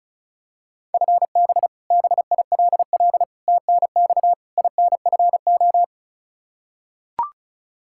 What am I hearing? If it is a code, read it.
Morse code, heard as a keyed tone, about 36 wpm, 700 Hz. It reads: FB BILL TNX INFO
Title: Practice QSO - Exchange 3 - More Info - 35wpm